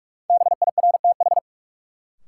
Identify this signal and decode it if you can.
Morse code, heard as a keyed tone, about 46 wpm, 700 Hz. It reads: BIRTH
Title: Sentences from Top 800 Words - New Word Drill - 45wpm